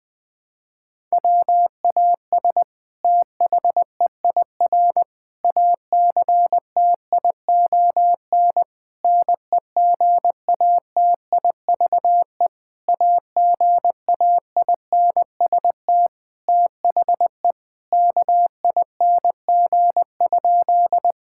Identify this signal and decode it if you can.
Morse code, heard as a keyed tone, about 21 wpm, 700 Hz. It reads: WAS THEIR ACTION NEGATIVE AGAINST THE KING?
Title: Sentences from Top 1000 Words - Review Part 6 - 20wpm